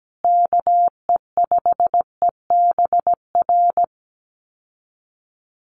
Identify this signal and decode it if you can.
Morse code, heard as a keyed tone, about 17 wpm, 700 Hz. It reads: KE5EBR
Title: USA Call Signs Repeated 2x 17wpm